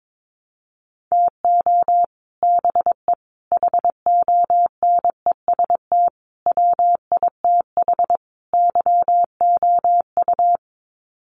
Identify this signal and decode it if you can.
Morse code, heard as a keyed tone, about 22 wpm, 700 Hz. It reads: TO BE HONEST WITH YOU
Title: Sets of 5 Words Encore 22wpm